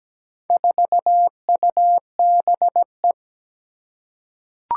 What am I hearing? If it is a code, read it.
Morse code, heard as a keyed tone, about 17 wpm, 700 Hz. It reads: 4UBE
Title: Four Challenging Letter-Number Combinations 17wpm